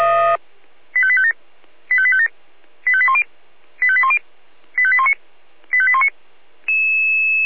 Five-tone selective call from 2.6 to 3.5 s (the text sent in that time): 85810